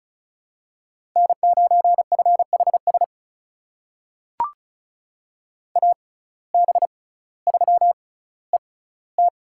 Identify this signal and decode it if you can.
Morse code, heard as a keyed tone, about 36 wpm, spaced as for 18 wpm, 700 Hz. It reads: N9FHS AB3ET
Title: USA Call Signs - 5 Characters - 6x Character Spacing - 35wpm